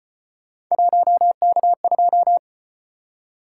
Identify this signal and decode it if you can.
Morse code, heard as a keyed tone, about 33 wpm, 700 Hz. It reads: 1K2